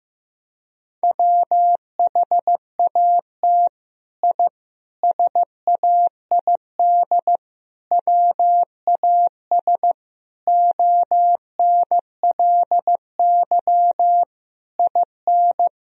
Morse code, heard as a keyed tone, about 15 wpm, 700 Hz. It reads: WHAT I SAID WAS ONLY IN
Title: Sentences from Top 1000 Words - Review Part 4 - 15wpm